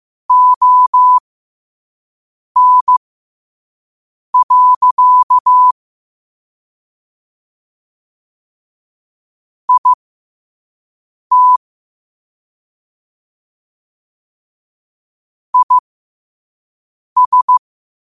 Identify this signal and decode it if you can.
Morse code, heard as a keyed tone, about 15 wpm, spaced as for 5 wpm, 1000 Hz. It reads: ON. IT IS
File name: qotdfiles / 5wpm